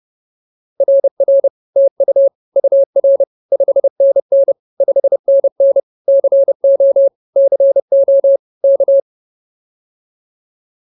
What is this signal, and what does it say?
Morse code, heard as a keyed tone, about 30 wpm, 550 Hz. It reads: RR TU UR 5NN 5NN CO CO K